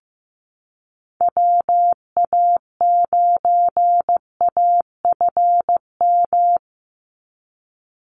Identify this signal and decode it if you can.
Morse code, heard as a keyed tone, about 15 wpm, 700 Hz. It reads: WA9AFM
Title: USA Call Signs 15wpm